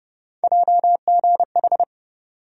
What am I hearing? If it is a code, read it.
Morse code, heard as a keyed tone, about 30 wpm, 700 Hz. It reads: JGH
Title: USA Call Sign Postfixes - 3 Letters 30wpm